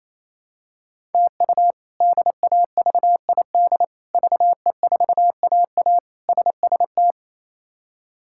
Morse code, heard as a keyed tone, about 28 wpm, 700 Hz. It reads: TU DAVID VE4AA SST
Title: K1USN SST QSO Practice - 28wpm